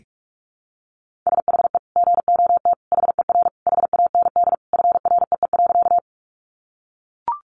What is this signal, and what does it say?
Morse code, heard as a keyed tone, about 44 wpm, 700 Hz. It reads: SHE GOT HER HAND FREE.
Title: Sentences from Top 500 Words 45wpm